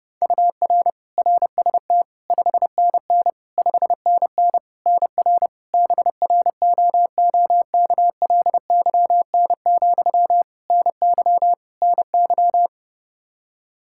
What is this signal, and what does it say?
Morse code, heard as a keyed tone, about 30 wpm, 700 Hz. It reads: UR RST 5NN 5NN NR BROOKLYN, NY NY
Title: K1USN SST QSO Practice 30wpm